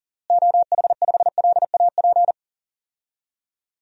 Morse code, heard as a keyed tone, about 39 wpm, 700 Hz. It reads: OH5LAP
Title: DXCC-100 Call Signs - 6 Characters - Repeated 3x - 40wpm